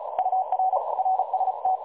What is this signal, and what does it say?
Morse code, heard as a keyed tone, about 35 wpm, 750 Hz, with noise. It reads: UR RST
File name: dlazaawansowanych / nagraniazpasma / websdr recording 3565.0kHz